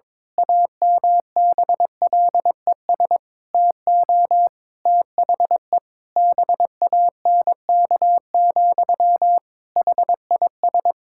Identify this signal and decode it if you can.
Morse code, heard as a keyed tone, about 22 wpm, 700 Hz. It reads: AMBLES TO THE BANK, HIS
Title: Pangram Challenge v2 - 22wpm